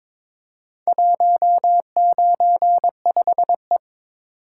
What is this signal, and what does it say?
Morse code, heard as a keyed tone, about 22 wpm, 700 Hz. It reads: 195E
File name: practice / 22wpm / Four Letter-Number Combinations - 22wpm